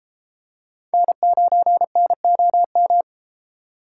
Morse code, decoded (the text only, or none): N9NOM